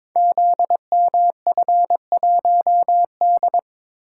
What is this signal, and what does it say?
Morse code, heard as a keyed tone, about 22 wpm, 700 Hz. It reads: ZMF1D
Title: Code Groups - 5 Letter:Numbers - 22wpm